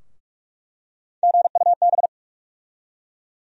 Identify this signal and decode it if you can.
Morse code, heard as a keyed tone, about 46 wpm, 700 Hz. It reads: GUB
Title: USA Call Sign Postfixes - 3 Letters 45wpm